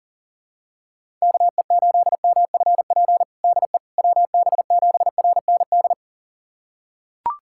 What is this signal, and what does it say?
Morse code, heard as a keyed tone, about 37 wpm, 700 Hz. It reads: KE8MFP DE WB7RND
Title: USA Callsign DE Callsign - 6 Characters - 40wpm